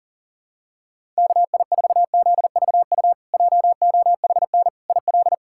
Morse code, heard as a keyed tone, about 40 wpm, 700 Hz. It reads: KI4ZVU JOHN IL
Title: K1USN SST QSO Practice - 40wpm